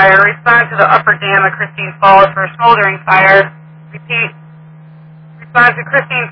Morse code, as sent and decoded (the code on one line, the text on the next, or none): none